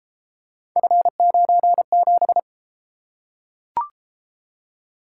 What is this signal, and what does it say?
Morse code, heard as a keyed tone, about 30 wpm, 700 Hz. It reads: F97